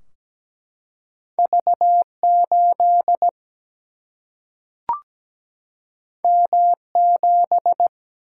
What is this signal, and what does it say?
Morse code, heard as a keyed tone, about 17 wpm, 700 Hz. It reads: V 8 M 7
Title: Single-Letter Single-Number 17wpm